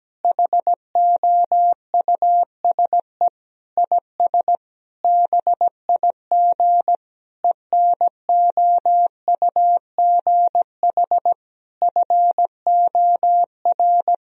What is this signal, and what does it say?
Morse code, heard as a keyed tone, about 17 wpm, 700 Hz. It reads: HOUSE IS BIG ENOUGH FOR
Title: Sentences from Top 1000 Words - Review Part 1 - 17wpm